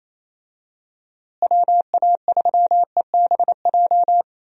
Morse code, heard as a keyed tone, about 28 wpm, 700 Hz. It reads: WA3EBJ
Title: USA Call Signs - 6 Characters - Repeated 3x - 28wpm